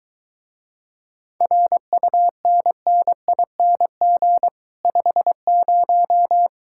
Morse code, heard as a keyed tone, about 22 wpm, 700 Hz. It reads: RUNNING 50